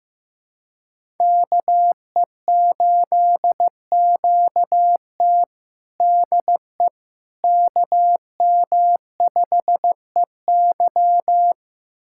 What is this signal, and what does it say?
Morse code, heard as a keyed tone, about 15 wpm, 700 Hz. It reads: KE8QT DE KM5EY
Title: USA Callsign DE Callsign - 5 Characters - 15wpm